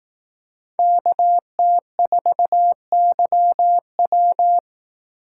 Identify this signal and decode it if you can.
Morse code, heard as a keyed tone, about 18 wpm, 700 Hz. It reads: KT4YW
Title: Speed Racing - USA 2x2 Call Signs 15wpm